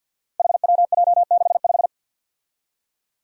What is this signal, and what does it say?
Morse code, heard as a keyed tone, about 51 wpm, 700 Hz. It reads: HWJ65
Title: Five Challenging Letter-Number Combinations 50wpm